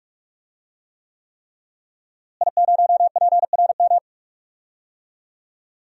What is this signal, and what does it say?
Morse code, heard as a keyed tone, about 41 wpm, 700 Hz. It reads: I0PRM
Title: DXCC-100 Call Signs - 5 Characters - Repeated 3x - 45wpm